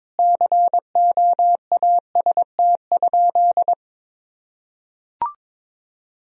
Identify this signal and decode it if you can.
Morse code, heard as a keyed tone, about 22 wpm, 700 Hz. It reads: COAST?
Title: Sentences from Top 1000 Words - Review Part 4 - 22wpm